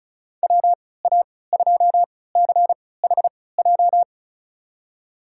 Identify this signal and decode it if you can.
Morse code, heard as a keyed tone, about 36 wpm, spaced as for 20 wpm, 700 Hz. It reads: WA2CHJ
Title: USA Call Signs - 6 Characters - 3x Character Spacing - 35wpm